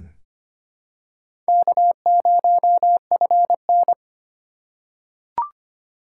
Morse code, heard as a keyed tone, about 25 wpm, 700 Hz. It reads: K0FN
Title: USA Call Signs - 2x Character Spacing - 25wpm